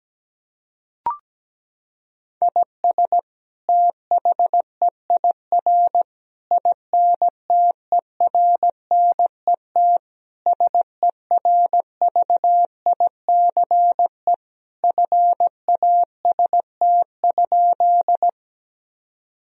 Morse code, heard as a keyed tone, about 17 wpm, 700 Hz. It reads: IS THEIR INTERNET SERVICE FAST?
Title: Sentences from Top 1000 Words - Review Part 4 - 17wpm